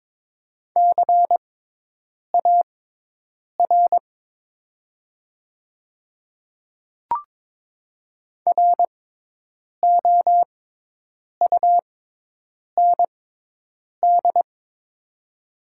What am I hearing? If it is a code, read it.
Morse code, heard as a keyed tone, about 22 wpm, spaced as for 8 wpm, 700 Hz. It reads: CAR ROUND
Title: 6x Character Spacing - Top 300 Words - 22wpm